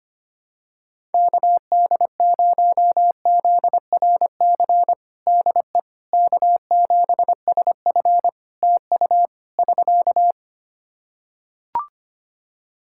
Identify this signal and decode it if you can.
Morse code, heard as a keyed tone, about 25 wpm, 700 Hz. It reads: KD0ZRC DE K7SF TU <SK>
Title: Practice QSO - Exchange 4 - Ending - 25wpm